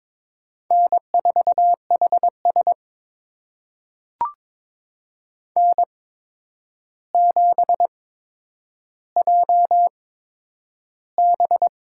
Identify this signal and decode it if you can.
Morse code, heard as a keyed tone, about 22 wpm, spaced as for 9 wpm, 700 Hz. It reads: N4HS N7JB
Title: USA Call Signs - 4 Characters - 8x Character Spacing - 22wpm